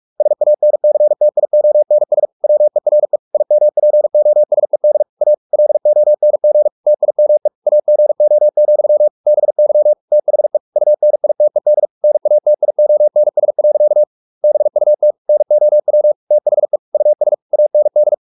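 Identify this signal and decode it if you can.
Morse code, heard as a keyed tone, about 44 wpm, 600 Hz. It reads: SANCTIONS WERE IMPOSED A LONG TIME AGO, BY THE UNITED NATIONS. BUT NOW THE US AND
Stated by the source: Morse practice at 45 wpm, the novel Never by Ken Follett in English, from CW med Gnister.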